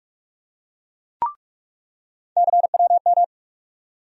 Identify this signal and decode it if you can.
Morse code, heard as a keyed tone, about 46 wpm, 700 Hz. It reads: CWM
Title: USA Call Sign Postfixes - 3 Letters 45wpm